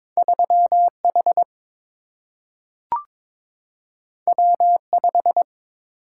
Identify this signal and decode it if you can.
Morse code, heard as a keyed tone, about 22 wpm, 700 Hz. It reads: 3 H W 5
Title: Single-Letter Single-Number 22wpm